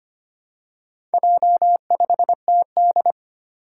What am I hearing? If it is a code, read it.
Morse code, heard as a keyed tone, about 25 wpm, 700 Hz. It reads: J5TD
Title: Four Challenging Letter-Number Combinations 25wpm